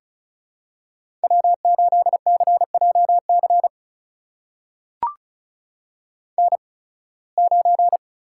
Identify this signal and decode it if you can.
Morse code, heard as a keyed tone, about 36 wpm, spaced as for 23 wpm, 700 Hz. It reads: W8CJC N9
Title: USA Call Signs - 5 Characters - 8x Character Spacing - 35wpm